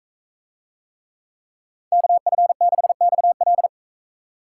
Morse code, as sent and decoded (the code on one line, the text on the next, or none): -.- ..-. -.... -..- .-..
KF6XL